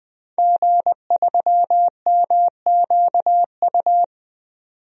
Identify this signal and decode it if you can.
Morse code, heard as a keyed tone, about 20 wpm, 700 Hz. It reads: G3MQU
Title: DXCC-100 Call Signs - 5 Characters - 20wpm